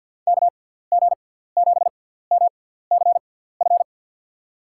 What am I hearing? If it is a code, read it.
Morse code, heard as a keyed tone, about 52 wpm, spaced as for 17 wpm, 700 Hz. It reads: KG7MCF
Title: USA Call Signs - 6 Characters - 6x Character Spacing - 50wpm